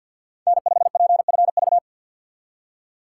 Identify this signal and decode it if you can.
Morse code, heard as a keyed tone, about 52 wpm, 690 Hz. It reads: N5PFV
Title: USA Call Signs - 5 Characters - 2x Character Spacing - 50wpm